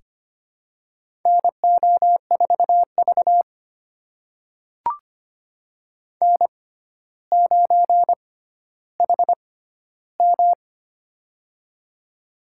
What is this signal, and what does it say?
Morse code, heard as a keyed tone, about 25 wpm, spaced as for 13 wpm, 700 Hz. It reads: NO4V N9HM
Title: USA Call Signs - 4 Characters - 6x Character Spacing - 25wpm